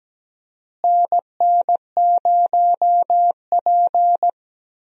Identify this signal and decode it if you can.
Morse code, heard as a keyed tone, about 17 wpm, 700 Hz. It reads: NN0P
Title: USA Call Signs - 4 Characters - Repeated 3x - 17wpm